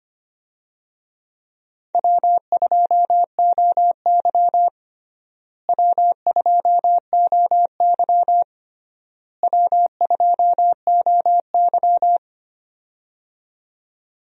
Morse code, heard as a keyed tone, about 25 wpm, 700 Hz. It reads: W2OY W2OY W2OY
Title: USA Call Signs - 4 Characters - Repeated 3x - 25wpm